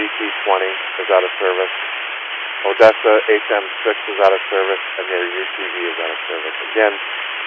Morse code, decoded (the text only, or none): none